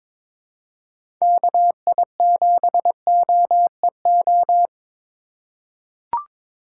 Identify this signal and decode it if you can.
Morse code, heard as a keyed tone, about 22 wpm, 700 Hz. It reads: KI7OEO
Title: USA Call Signs - 6 Characters - 3x Character Spacing - 22wpm